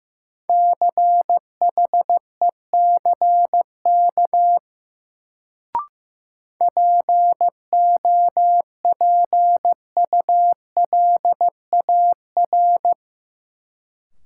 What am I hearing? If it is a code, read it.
Morse code, heard as a keyed tone, about 15 wpm, 700 Hz. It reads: CHECK POPULAR
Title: Sentences from Top 1000 Words - Review Part 3 - New Word Drill - 15wpm